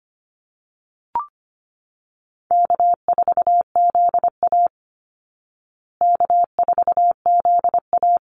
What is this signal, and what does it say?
Morse code, heard as a keyed tone, about 25 wpm, 700 Hz. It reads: K4ZA K4ZA
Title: Easy Call Signs Repeated 2x - 25wpm